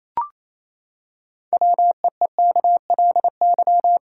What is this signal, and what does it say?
Morse code, heard as a keyed tone, about 28 wpm, 700 Hz. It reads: WEEKLY
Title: Pangram Challenge v2 - 28wpm